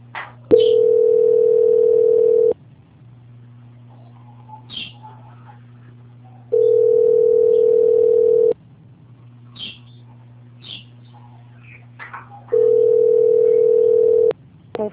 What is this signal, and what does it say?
Ringback tone (North America), 440 + 480 Hz.
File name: out- -2047- - - .4706